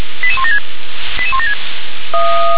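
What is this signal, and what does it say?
Five-tone selective call (ZVEI), heard as quick runs of five tones, tones of about 70 ms. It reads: 88176 88176
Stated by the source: Five-tone ZVEI selective call received on an SDR.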